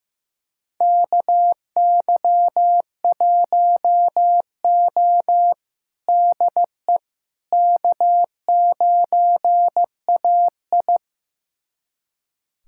Morse code, heard as a keyed tone, about 15 wpm, 700 Hz. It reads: KY1O DE K9AI
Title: USA Callsign DE Callsign - 4 Characters - 15wpm